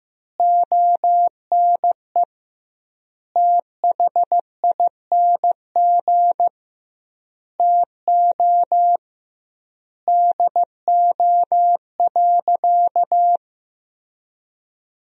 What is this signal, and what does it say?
Morse code, heard as a keyed tone, about 15 wpm, 700 Hz. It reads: ONE THING TO DO.
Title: 2x Word-Spacing - Sentences from Top 100 Words - 15wpm